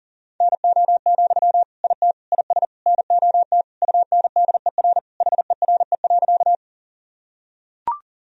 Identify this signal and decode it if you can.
Morse code, heard as a keyed tone, about 40 wpm, 700 Hz. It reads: NO, IT IS NOT UNDER HERE.
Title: Sentences from Top 1000 Words - Review Part 1 - 40wpm